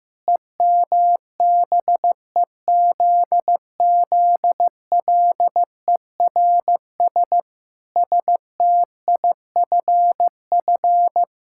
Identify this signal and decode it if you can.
Morse code, heard as a keyed tone, about 15 wpm, 700 Hz. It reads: EMBEZZLERS STIFF
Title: Pangram Challenge v2 - 15wpm